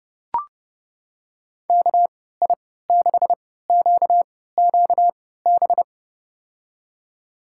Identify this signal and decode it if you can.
Morse code, heard as a keyed tone, about 30 wpm, spaced as for 17 wpm, 700 Hz. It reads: KI6QQB
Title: USA Call Signs - 6 Characters - 3x Character Spacing - 30wpm